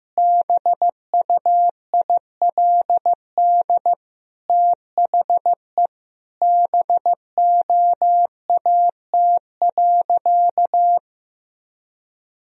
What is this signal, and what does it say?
Morse code, heard as a keyed tone, about 15 wpm, 700 Hz. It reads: BUILD THE BOAT.
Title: Sentences from Top 1000 Words - Review Part 1 - 15wpm